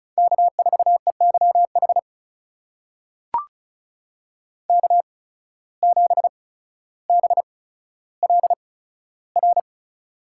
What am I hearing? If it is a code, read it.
Morse code, heard as a keyed tone, about 36 wpm, spaced as for 15 wpm, 700 Hz. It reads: K4EYH K7BLR
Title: USA Call Signs - 5 Characters - 8x Character Spacing - 35wpm